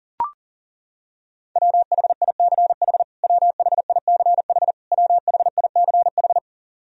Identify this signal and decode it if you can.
Morse code, heard as a keyed tone, about 40 wpm, 700 Hz. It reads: WHICH WHICH WHICH
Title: Top 100 Words Repeated 40wpm